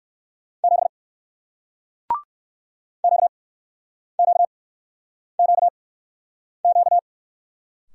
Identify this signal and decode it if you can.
Morse code, heard as a keyed tone, about 61 wpm, spaced as for 9 wpm, 700 Hz. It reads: 7 QQQQ